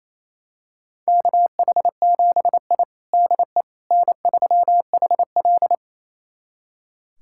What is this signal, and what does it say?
Morse code, heard as a keyed tone, about 28 wpm, 700 Hz. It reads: KH7I DE N3HL
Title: USA Callsign DE Callsign - 4 Characters - 28wpm